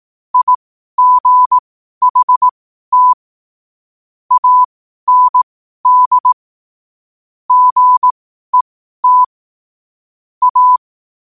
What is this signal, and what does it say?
Morse code, heard as a keyed tone, about 18 wpm, spaced as for 12 wpm, 1000 Hz. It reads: IGHT AND GET A
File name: qotdfiles / 12wpm